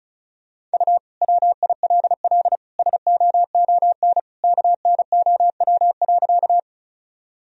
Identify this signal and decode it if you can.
Morse code, heard as a keyed tone, about 36 wpm, 700 Hz. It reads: U WILL SOON KNOW.
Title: Sentences from Top 400 Words and Abbreviations 35wpm